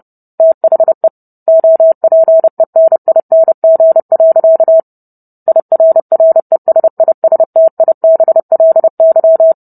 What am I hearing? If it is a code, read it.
Morse code, heard as a keyed tone, about 30 wpm, 650 Hz. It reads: THE OPENING. IRRESISTIBLY